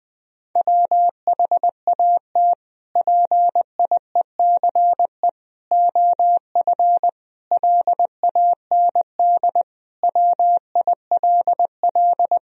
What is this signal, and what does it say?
Morse code, heard as a keyed tone, about 20 wpm, 700 Hz. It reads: WHAT PIECE OF LAND WILL
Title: Sentences from Top 1000 Words - Review Part 2 - 20wpm